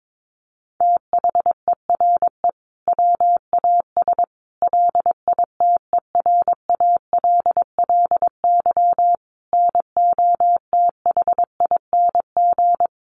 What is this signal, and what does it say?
Morse code, heard as a keyed tone, about 22 wpm, 700 Hz. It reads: THERE WAS LITERALLY NOTHING
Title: Sentences from Top 1400 Words 22wpm